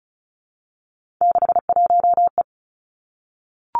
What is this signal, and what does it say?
Morse code, heard as a keyed tone, about 36 wpm, 700 Hz. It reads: 61E
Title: Three Challenging Letter-Number Combinations 35wpm